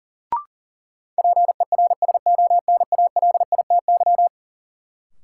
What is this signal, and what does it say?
Morse code, heard as a keyed tone, about 38 wpm, 700 Hz. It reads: PERSONALITY
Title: Sentences from Top 1400 Words - New Word Drill Alt - 40wpm